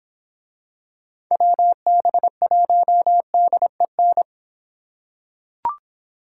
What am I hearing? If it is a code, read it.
Morse code, heard as a keyed tone, about 26 wpm, 700 Hz. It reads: WB1DEN